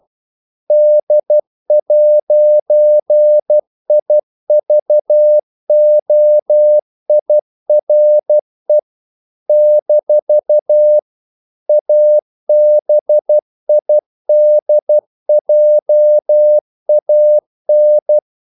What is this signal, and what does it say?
Morse code, heard as a keyed tone, about 12 wpm, 600 Hz. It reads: D'IVOIRE - ABIDJAN
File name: CW / Practice / NationalCapitals12